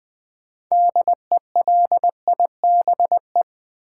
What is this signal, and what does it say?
Morse code, heard as a keyed tone, about 20 wpm, 700 Hz. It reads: DELIBE
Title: Build A Word - 6 Letters At A Time - 12-Letter Words - 20wpm